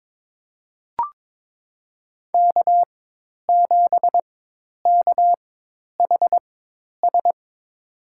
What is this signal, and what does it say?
Morse code, heard as a keyed tone, about 22 wpm, spaced as for 10 wpm, 700 Hz. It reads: K7KHS
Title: USA Call Signs - 5 Characters - 4x Character Spacing - 22wpm